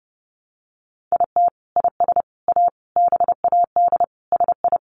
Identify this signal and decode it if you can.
Morse code, heard as a keyed tone, about 30 wpm, 700 Hz. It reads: IT IS A BAD SI
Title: Sentences from Top 1000 Words 30wpm